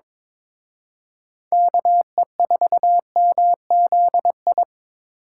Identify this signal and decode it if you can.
Morse code, heard as a keyed tone, about 25 wpm, 700 Hz. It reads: KE4MZI
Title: Speed Racing - USA 2x3 Call Signs 22wpm